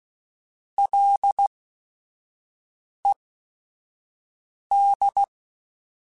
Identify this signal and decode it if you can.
Morse code, heard as a keyed tone, about 16 wpm, spaced as for 5 wpm, 800 Hz. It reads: LED
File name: Ls / Single L m3s / Single L @5